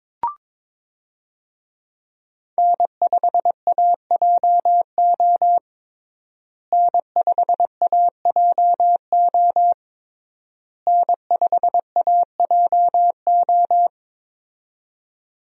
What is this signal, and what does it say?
Morse code, heard as a keyed tone, about 22 wpm, 700 Hz. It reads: N5AJO N5AJO N5AJO
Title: USA Call Signs - 5 Characters - Repeated 3x - 22wpm